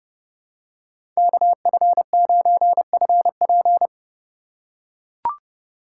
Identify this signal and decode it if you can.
Morse code, heard as a keyed tone, about 30 wpm, 700 Hz. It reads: KF9FP
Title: USA Call Signs - 5 Characters - 2x Character Spacing - 30wpm